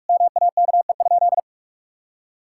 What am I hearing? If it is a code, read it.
Morse code, heard as a keyed tone, about 41 wpm, 700 Hz. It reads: MAKE?
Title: Sentences from Top 1000 Words - Review Part 5 - 45wpm